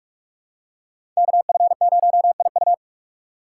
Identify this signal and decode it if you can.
Morse code, heard as a keyed tone, about 44 wpm, 700 Hz. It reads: KF0IU
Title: USA Call Signs - 5 Characters - 3x Character Spacing - 45wpm